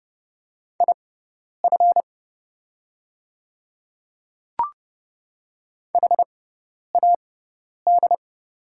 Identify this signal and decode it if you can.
Morse code, heard as a keyed tone, about 30 wpm, spaced as for 10 wpm, 700 Hz. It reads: IF HAD
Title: 6x Character Spacing - Top 100 Words - 30wpm